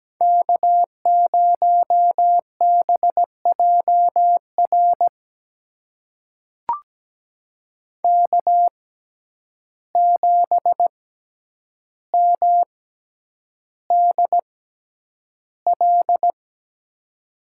Morse code, heard as a keyed tone, about 17 wpm, spaced as for 9 wpm, 700 Hz. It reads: K0BJR K7MDL
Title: USA Call Signs - 5 Characters - 6x Character Spacing - 17wpm